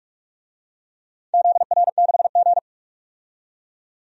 Morse code, heard as a keyed tone, about 41 wpm, 700 Hz. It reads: ZR6G
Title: DXCC-100 Call Signs - 4 Characters - 45wpm